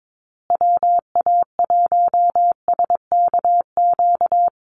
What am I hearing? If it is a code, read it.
Morse code, heard as a keyed tone, about 22 wpm, 700 Hz. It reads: WA1SKQ
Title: Call Signs Repeated 2x - 22wpm